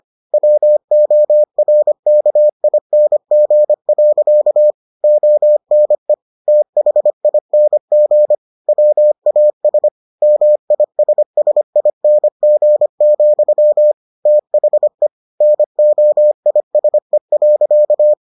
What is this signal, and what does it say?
Morse code, heard as a keyed tone, about 25 wpm, 600 Hz. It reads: WORKING. ONE THING WAS MISSING, THE NOISE.